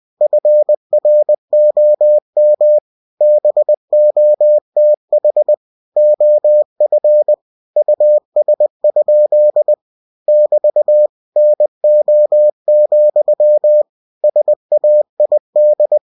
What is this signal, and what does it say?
Morse code, heard as a keyed tone, about 20 wpm, 600 Hz. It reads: FROM BOTH OF US? <BT> NO, SAID
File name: cw-books / winnie-the-pooh / winnie-20wpm-page-0051